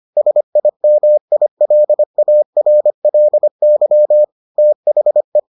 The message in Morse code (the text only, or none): SIMILARLY THE